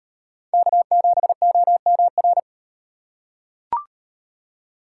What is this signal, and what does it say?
Morse code, heard as a keyed tone, about 38 wpm, 700 Hz. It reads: K7OMR